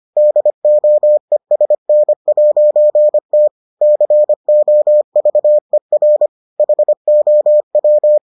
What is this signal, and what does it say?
Morse code, heard as a keyed tone, about 25 wpm, 600 Hz. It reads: DOESN'T COVER HOW